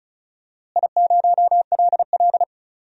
Morse code, heard as a keyed tone, about 36 wpm, 700 Hz. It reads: I0LL
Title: DXCC-100 Call Signs - 4 Characters - Repeated 2x - 35wpm